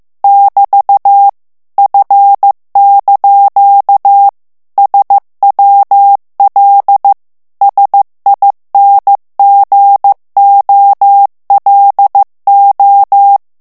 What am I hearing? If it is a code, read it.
Morse code, heard as a keyed tone, about 15 wpm, 800 Hz. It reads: <BT> F) SWL SINGOLO
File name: morse mp3 / ITALIEN texte clair 2 vitesse 14 mots minute